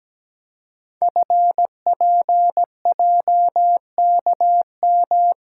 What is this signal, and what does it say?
Morse code, heard as a keyed tone, about 17 wpm, 700 Hz. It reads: FPJKM
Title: Code Groups - 5 Letter:Numbers - 17wpm